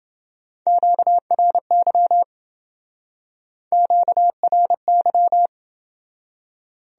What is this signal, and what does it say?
Morse code, heard as a keyed tone, about 28 wpm, 700 Hz. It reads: QRY QRY